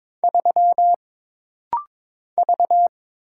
Morse code, heard as a keyed tone, about 22 wpm, 700 Hz. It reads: 3 V
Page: This rapid-fire timing is geared to push your ICR (Instant Character Recognition) skills.